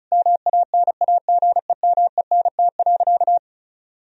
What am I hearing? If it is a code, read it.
Morse code, heard as a keyed tone, about 34 wpm, 700 Hz. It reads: MANAGEMENT.
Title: Sentences from Top 1000 Words - Review Part 4 - 35wpm